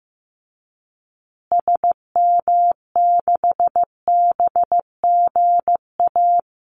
Morse code, heard as a keyed tone, about 15 wpm, 700 Hz. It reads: SM6BGA
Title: Call Signs Repeated 2x - 15wpm